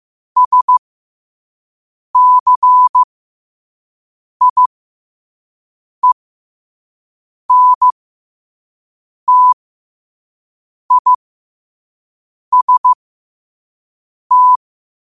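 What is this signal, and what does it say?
Morse code, heard as a keyed tone, about 15 wpm, spaced as for 5 wpm, 1000 Hz. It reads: SCIENTIST
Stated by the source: Morse at 5 wpm.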